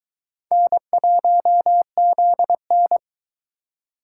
Morse code, heard as a keyed tone, about 22 wpm, 700 Hz. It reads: N1ZN